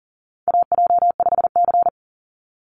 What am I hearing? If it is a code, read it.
Morse code, heard as a keyed tone, about 37 wpm, 700 Hz. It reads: AJ5C
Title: NA SOTA Chaser Callsigns 40wpm